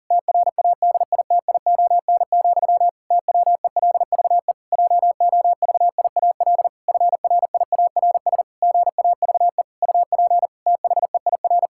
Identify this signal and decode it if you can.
Morse code, heard as a keyed tone, about 40 wpm, 700 Hz. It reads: TRADITION, TWELVE JOVIAL FRIARS GAVE UP THEIR
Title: Pangram Challenge v2 - 40wpm